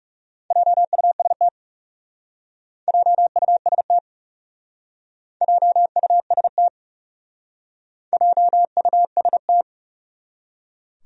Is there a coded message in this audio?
Yes: Morse code from 0.5 to 9.6 s, heard as a keyed tone, about 39 wpm, 700 Hz. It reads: JUST JUST JUST JUST